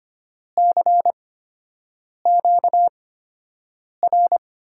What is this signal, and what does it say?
Morse code, heard as a keyed tone, about 25 wpm, spaced as for 7 wpm, 700 Hz. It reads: CQR
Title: USA Call Signs - 6 Characters - 8x Character Spacing - 25wpm